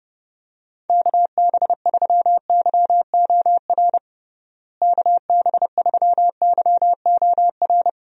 Morse code, heard as a keyed tone, about 30 wpm, 700 Hz. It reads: KB3YOR KB3YOR
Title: USA Call Signs - 6 Characters - Repeated 2x - 30wpm